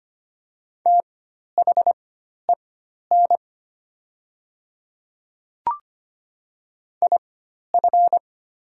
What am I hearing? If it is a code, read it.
Morse code, heard as a keyed tone, about 25 wpm, spaced as for 12 wpm, 700 Hz. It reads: THEN IF
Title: 4x Character Spacing - Top 100 Words - 25wpm